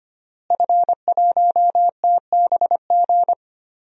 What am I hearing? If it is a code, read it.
Morse code, heard as a keyed tone, about 25 wpm, 700 Hz. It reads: F1TBG
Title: DXCC-100 Call Signs - 5 Characters - Repeated 3x - 25wpm